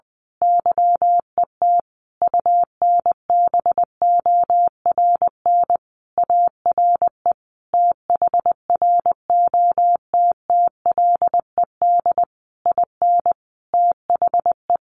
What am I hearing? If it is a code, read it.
Morse code, heard as a keyed tone, about 20 wpm, 700 Hz. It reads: YET UNBORN ARE THROTTLED IN THE
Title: A Sound Of Thunder 20wpm